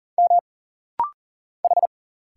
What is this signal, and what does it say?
Morse code, heard as a keyed tone, about 40 wpm, 700 Hz. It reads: M H
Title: Single Letters - Rapid-Fire 40wpm